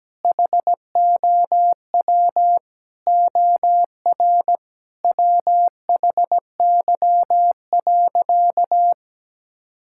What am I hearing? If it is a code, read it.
Morse code, heard as a keyed tone, about 17 wpm, 700 Hz. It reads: HOW OR WHY.
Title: Sentences from Top 1000 Words - Review Part 6 - 17wpm